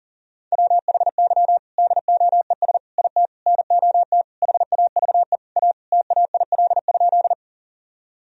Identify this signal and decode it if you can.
Morse code, heard as a keyed tone, about 37 wpm, 700 Hz. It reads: WHY DOES IT NOT HAVE A TAIL?
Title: Sentences from Top 1000 Words - Review Part 3 - 40wpm